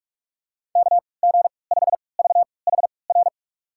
Morse code, heard as a keyed tone, about 46 wpm, spaced as for 25 wpm, 700 Hz. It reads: KG5VHR
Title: USA Call Signs - 6 Characters - 3x Character Spacing - 45wpm